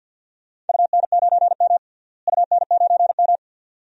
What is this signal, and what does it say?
Morse code, heard as a keyed tone, about 47 wpm, 700 Hz. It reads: UN9M UN9M
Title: DXCC-100 Call Signs - 4 Characters - Repeated 2x - 50wpm